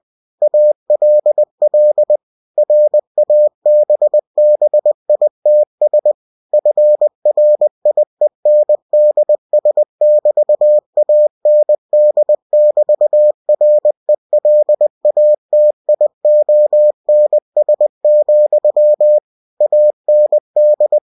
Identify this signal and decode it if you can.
Morse code, heard as a keyed tone, about 20 wpm, 600 Hz. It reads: ALL RABBITS FRIENDS<BT>AND<BT>RELATIONS, AND